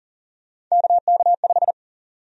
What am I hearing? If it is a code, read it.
Morse code, heard as a keyed tone, about 40 wpm, 700 Hz. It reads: KK5
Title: USA Call Sign Prefixes - Repeated 2x - 40wpm